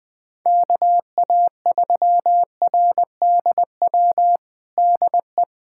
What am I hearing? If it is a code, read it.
Morse code, heard as a keyed tone, about 20 wpm, 700 Hz. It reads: KA3RDW DE
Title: USA Callsign DE Callsign 20wpm